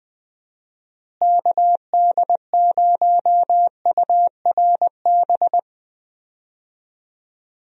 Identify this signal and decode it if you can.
Morse code, heard as a keyed tone, about 20 wpm, 700 Hz. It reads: KD0URB
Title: USA Call Signs - 6 Characters - Repeated 3x - 20wpm